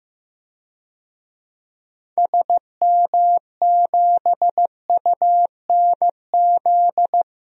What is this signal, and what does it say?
Morse code, heard as a keyed tone, about 15 wpm, 700 Hz. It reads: SM7UNZ
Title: DXCC-100 Call Signs - 6 Characters - Repeated 2x - 15wpm